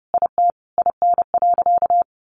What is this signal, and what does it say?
Morse code, heard as a keyed tone, about 30 wpm, 700 Hz. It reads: IT IN.
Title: Sentences from Top 1300 Words 30wpm